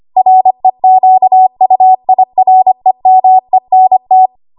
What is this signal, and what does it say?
Morse code, heard as a keyed tone, about 25 wpm, 750 Hz. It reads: REQUIREMENT
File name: Archive / 25 WPM / 25WPM